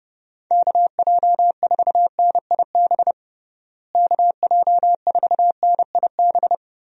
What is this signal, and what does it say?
Morse code, heard as a keyed tone, about 32 wpm, 700 Hz. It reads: KJ4NIB KJ4NIB
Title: USA Call Signs - 6 Characters - Repeated 3x - 30wpm